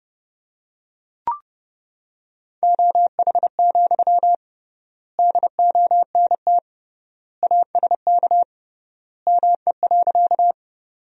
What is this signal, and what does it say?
Morse code, heard as a keyed tone, about 30 wpm, 700 Hz. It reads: OH, DONT ASK ME.
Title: 3x Word-Spacing - Sentences from Top 500 Words - 30wpm